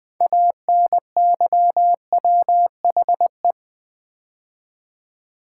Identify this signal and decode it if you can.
Morse code, heard as a keyed tone, about 20 wpm, 700 Hz. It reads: ANYWHE
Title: Build A Word - 2 Letters At A Time - 8-Letter Words - 20wpm